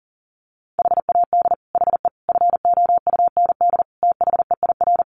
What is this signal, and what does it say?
Morse code, heard as a keyed tone, about 37 wpm, 700 Hz. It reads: HAD HE FOUND THEIR
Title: Sentences from Top 1300 Words 40wpm